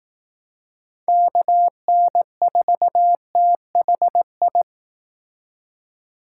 Morse code, heard as a keyed tone, about 18 wpm, 700 Hz. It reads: KN4THI
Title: Speed Racing - USA 2x3 Call Signs 15wpm